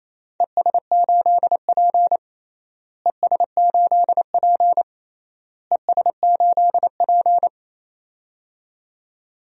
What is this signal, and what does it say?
Morse code, heard as a keyed tone, about 28 wpm, 700 Hz. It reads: ES8P ES8P ES8P
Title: DXCC-100 Call Signs - 4 Characters - Repeated 3x - 28wpm